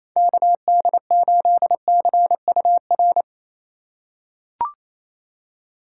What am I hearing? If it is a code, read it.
Morse code, heard as a keyed tone, about 28 wpm, 700 Hz. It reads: KD8CUR
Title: USA Call Signs - 6 Characters - 4x Character Spacing - 28wpm